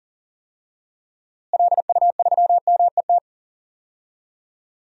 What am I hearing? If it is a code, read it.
Morse code, heard as a keyed tone, about 40 wpm, 700 Hz. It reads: LU3MET
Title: DXCC-100 Call Signs - 6 Characters - Repeated 3x - 40wpm